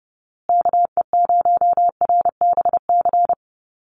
Morse code, heard as a keyed tone, about 30 wpm, 700 Hz. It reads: KE0RBC
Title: USA Call Signs Repeated 2x 30wpm